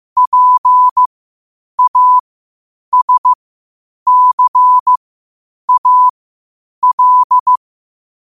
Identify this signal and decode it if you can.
Morse code, heard as a keyed tone, about 15 wpm, spaced as for 8 wpm, 1000 Hz. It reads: PASCAL
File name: qotdfiles / 8wpm